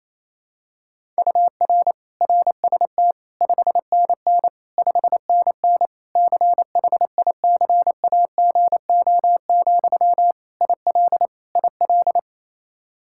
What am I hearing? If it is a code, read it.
Morse code, heard as a keyed tone, about 28 wpm, 700 Hz. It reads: UR RST 5NN 5NN CHICAGO, IL IL
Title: K1USN SST QSO Practice 28wpm